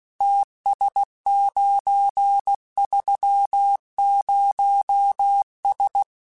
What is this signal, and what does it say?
Morse code, heard as a keyed tone, about 16 wpm, 800 Hz. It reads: TS930S
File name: QSOs / QSOs-mp3s / WA7QOX 2@15